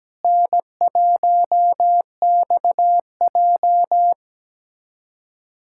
Morse code, heard as a keyed tone, about 17 wpm, 700 Hz. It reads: N1XJ
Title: USA Callsign DE Callsign - 4 Characters - 17wpm